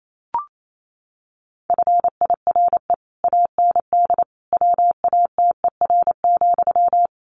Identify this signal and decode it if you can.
Morse code, heard as a keyed tone, about 28 wpm, 700 Hz. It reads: FIRE AND WATER,
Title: Sentences from Top 400 Words 28wpm